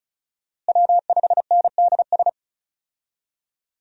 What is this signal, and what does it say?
Morse code, heard as a keyed tone, about 34 wpm, 700 Hz. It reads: W5NDS
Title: USA Call Signs - 5 Characters - Repeated 2x - 35wpm